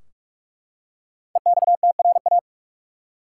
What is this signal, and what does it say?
Morse code, heard as a keyed tone, about 44 wpm, 700 Hz. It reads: EXTRA
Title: Five Letter Words - 45wpm